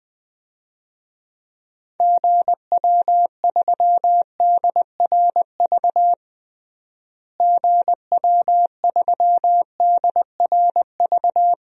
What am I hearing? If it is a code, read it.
Morse code, heard as a keyed tone, about 21 wpm, 700 Hz. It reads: GW3DRV GW3DRV
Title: DXCC-100 Call Signs - 6 Characters - Repeated 3x - 20wpm